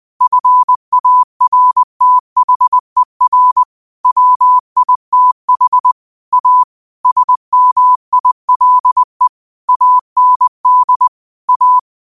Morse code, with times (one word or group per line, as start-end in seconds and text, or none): 0.2-3.6 FARTHER
4.1-5.9 WITH
6.3-6.6 A
7.1-9.3 SMILE
9.7-11.1 AND
11.5-11.8 A